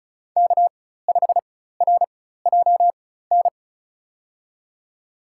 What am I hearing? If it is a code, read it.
Morse code, heard as a keyed tone, about 36 wpm, spaced as for 16 wpm, 700 Hz. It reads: K5RJN
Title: USA Call Signs - 5 Characters - 4x Character Spacing - 35wpm